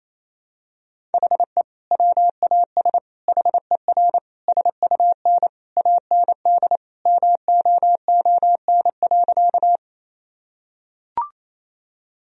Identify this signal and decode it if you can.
Morse code, heard as a keyed tone, about 28 wpm, 700 Hz. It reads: HE WAS HER SUN AND MOON.
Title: Sentences from Top 1000 Words - Review Part 3 - 28wpm